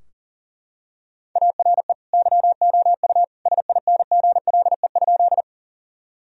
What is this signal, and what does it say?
Morse code, heard as a keyed tone, about 40 wpm, 700 Hz. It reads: ARE YOU SINGLE?
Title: Sentences from Top 1000 Words - Review Part 4 - 40wpm